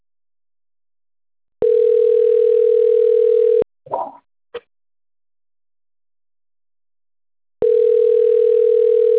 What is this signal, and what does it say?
Ringback tone (North America), 440 + 480 Hz.
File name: out- - - - - .32473